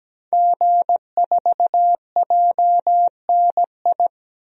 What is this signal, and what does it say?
Morse code, heard as a keyed tone, about 17 wpm, 700 Hz. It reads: G4JNI
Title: DXCC-100 Call Signs - 5 Characters - 17wpm